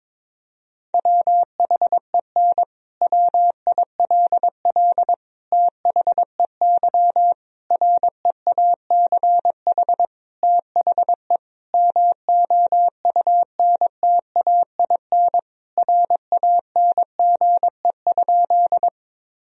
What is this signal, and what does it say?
Morse code, heard as a keyed tone, about 22 wpm, 700 Hz. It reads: WHEN WILL THEY REACH THE MOUNTAIN RANGE?
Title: Sentences from Top 1000 Words - Review Part 4 - 22wpm